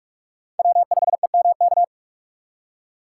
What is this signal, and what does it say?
Morse code, heard as a keyed tone, about 42 wpm, 700 Hz. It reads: W5EMK
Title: USA Call Signs - 5 Characters - 4x Character Spacing - 45wpm